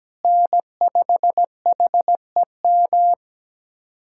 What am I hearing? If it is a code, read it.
Morse code, heard as a keyed tone, about 17 wpm, 700 Hz. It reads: N5HEM
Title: USA Callsign DE Callsign - 5 Characters - 17wpm